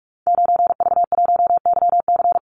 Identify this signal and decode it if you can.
Morse code, heard as a keyed tone, about 44 wpm, 700 Hz. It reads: 9V1YC
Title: Call Signs - 45wpm